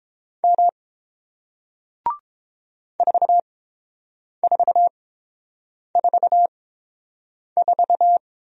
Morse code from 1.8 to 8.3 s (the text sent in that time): 4444